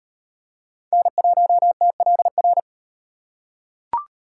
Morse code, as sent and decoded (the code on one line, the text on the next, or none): -. .---- - .-.. .-.
N1TLR